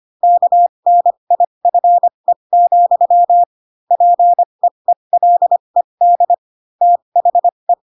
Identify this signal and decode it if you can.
Morse code, heard as a keyed tone, about 25 wpm, spaced as for 22 wpm, 700 Hz. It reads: KNIFE, PEELED THE
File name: Pinocchio / Pinocchio ch7 18 25 ik1hge